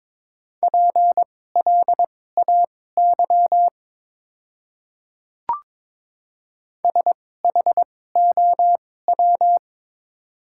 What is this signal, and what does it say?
Morse code, heard as a keyed tone, about 22 wpm, spaced as for 16 wpm, 700 Hz. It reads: PLAY SHOW
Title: 2x Character Spacing - Top 200 Words - 22wpm